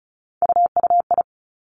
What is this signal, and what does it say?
Morse code, heard as a keyed tone, about 36 wpm, 700 Hz. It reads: UUI
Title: Three Challenging Letter-Number Combinations 35wpm